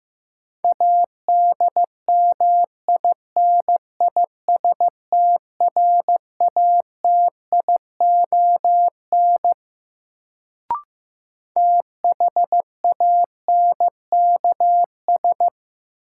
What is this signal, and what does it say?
Morse code, heard as a keyed tone, about 15 wpm, 700 Hz. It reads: ADMINISTRATION THANKS
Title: Sets of 5 Words - New Word Drill - 15wpm